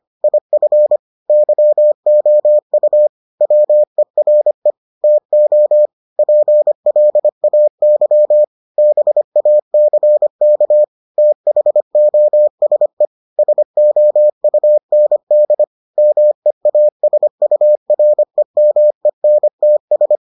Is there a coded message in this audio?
Yes: Morse code from 0.2 to 20.2 s, heard as a keyed tone, about 25 wpm, 600 Hz. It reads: IF YOU WERE TO PLAY BACK THOSE SOUND MEASUREMENTS